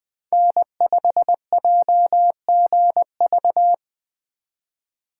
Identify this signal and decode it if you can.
Morse code, heard as a keyed tone, about 20 wpm, 700 Hz. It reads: N5JGV
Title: USA Call Signs - 5 Characters - 4x Character Spacing - 20wpm